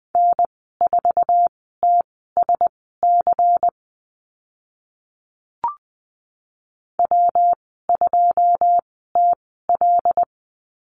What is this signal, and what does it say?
Morse code, heard as a keyed tone, about 20 wpm, spaced as for 14 wpm, 700 Hz. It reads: N4TSC W2TL
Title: 213 Lesson 34g - Number 8 - Call Signs 20wpm